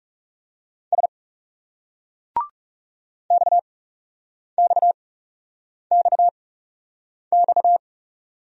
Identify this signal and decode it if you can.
Morse code, heard as a keyed tone, about 37 wpm, spaced as for 8 wpm, 700 Hz. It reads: S XXXX
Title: Speed-Racing Single Letters 30wpm